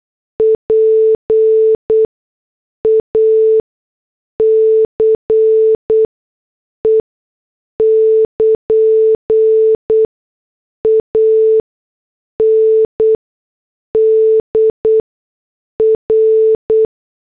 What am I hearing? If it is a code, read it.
Morse code, heard as a keyed tone, about 8 wpm, spaced as for 6 wpm, 440 Hz. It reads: PACE <KN> ANDR
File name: morse / headlines